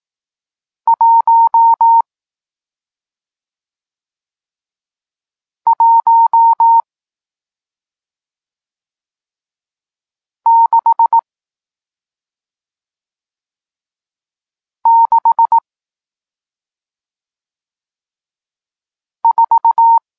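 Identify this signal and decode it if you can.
Morse code, heard as a keyed tone, about 18 wpm, spaced as for 2 wpm, 920 Hz. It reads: 11664
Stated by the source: Morse code lesson teaching digits 1, 4 and 6, from SP9MOA, Niepołomicki Klub Krótkofalowców (Poland) amateur radio club.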